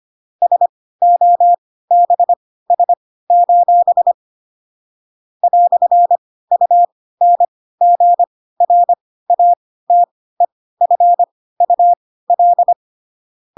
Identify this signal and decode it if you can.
Morse code, heard as a keyed tone, about 25 wpm, spaced as for 16 wpm, 700 Hz. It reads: SOBS: "UNGRATEFUL
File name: Pinocchio / Pinocchio ch3 10 25 ik1hge